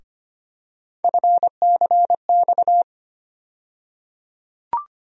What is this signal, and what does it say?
Morse code, heard as a keyed tone, about 25 wpm, 700 Hz. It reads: FCX
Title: USA Call Sign Postfixes - 3 Letters 25wpm